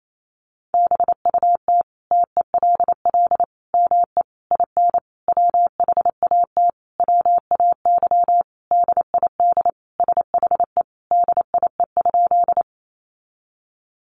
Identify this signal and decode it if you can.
Morse code, heard as a keyed tone, about 28 wpm, 700 Hz. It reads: BUT TELL ME IN WHAT WAY DID SHE DIE?
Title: Sentences from Top 1300 Words 28wpm